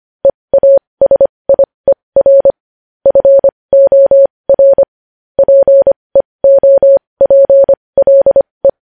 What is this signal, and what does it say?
Morse code, heard as a keyed tone, about 25 wpm, spaced as for 20 wpm, 550 Hz. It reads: EASIER FOR PEOPLE
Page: News Headlines in Morse Code at 20 WPM